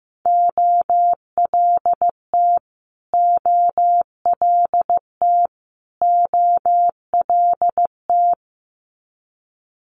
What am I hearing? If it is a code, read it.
Morse code, heard as a keyed tone, about 15 wpm, 700 Hz. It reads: OLT OLT OLT
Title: Multi-letter Phonemes Repeated 3x 15wpm